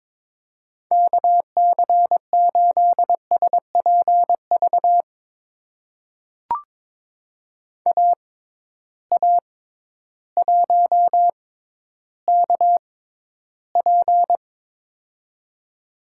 Morse code, heard as a keyed tone, about 22 wpm, spaced as for 12 wpm, 700 Hz. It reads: KC8SPV AA1KP
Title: USA Call Signs - 6x Character Spacing - 22wpm